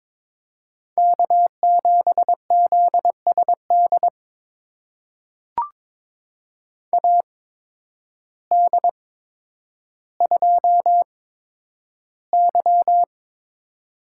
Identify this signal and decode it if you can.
Morse code, heard as a keyed tone, about 22 wpm, spaced as for 10 wpm, 700 Hz. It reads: K7ZSD AD2Y
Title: USA Call Signs - 8x Character Spacing - 22wpm